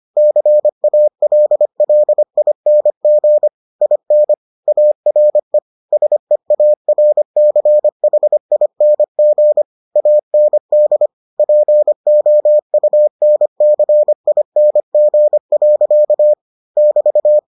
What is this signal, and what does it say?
Morse code, heard as a keyed tone, about 25 wpm, 600 Hz. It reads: CALLING IN ARE SEARCHING AND POUNCING. <BT>